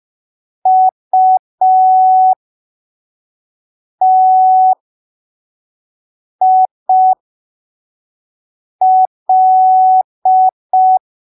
Morse code, heard as a keyed tone, about 5 wpm, spaced as for 3 wpm, 750 Hz. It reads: UTIL
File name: arquivos / texto / texto 06 05ppm